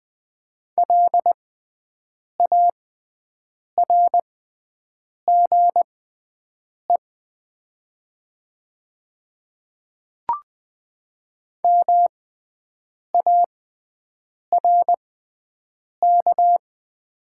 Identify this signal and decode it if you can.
Morse code, heard as a keyed tone, about 20 wpm, spaced as for 7 wpm, 700 Hz. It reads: LARGE MARK
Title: 6x Character Spacing - Top 500 Words - 20wpm